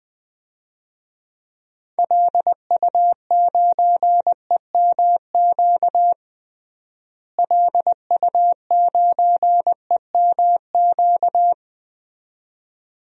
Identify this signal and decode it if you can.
Morse code, heard as a keyed tone, about 20 wpm, 700 Hz. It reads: LU9EMQ LU9EMQ
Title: DXCC-100 Call Signs - 6 Characters - Repeated 2x - 20wpm